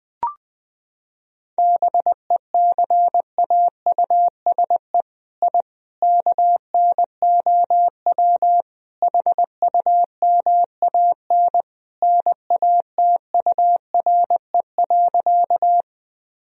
Morse code, heard as a keyed tone, about 20 wpm, 700 Hz. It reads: BECAUSE I KNOW HUMAN NATURE.
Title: Sentences from Top 1000 Words - Review Part 3 - 20wpm